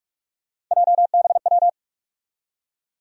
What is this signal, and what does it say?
Morse code, heard as a keyed tone, about 43 wpm, 700 Hz. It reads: JBW
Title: USA Call Sign Postfixes - 3 Letters 45wpm